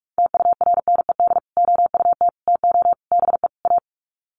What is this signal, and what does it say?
Morse code, heard as a keyed tone, about 44 wpm, 700 Hz. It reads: TURNED OUT TO BE A
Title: Sets of 5 Words 45wpm